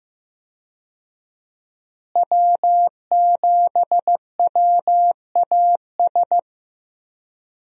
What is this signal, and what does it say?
Morse code, heard as a keyed tone, about 15 wpm, 700 Hz. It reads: W7WAS
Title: USA Call Signs - 5 Characters - Repeated 3x - 15wpm